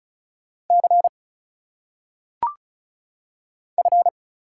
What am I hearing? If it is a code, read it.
Morse code, heard as a keyed tone, about 34 wpm, 700 Hz. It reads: C F
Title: Single Letters - Phonetic Pronunciation 35wpm